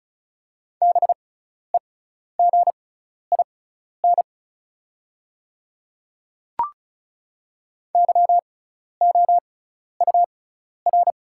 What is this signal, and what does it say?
Morse code, heard as a keyed tone, about 36 wpm, spaced as for 12 wpm, 690 Hz. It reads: BEGIN YOUR
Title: 6x Character Spacing - Top 300 Words - 35wpm